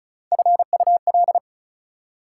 Morse code, heard as a keyed tone, about 34 wpm, 700 Hz. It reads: FUL
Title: Prefixes and Suffixes 35wpm